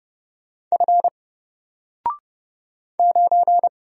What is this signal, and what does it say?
Morse code, heard as a keyed tone, about 30 wpm, 700 Hz. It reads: F 9
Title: Single Character (Letter:Number:Punctuation:Prosign) 30wpm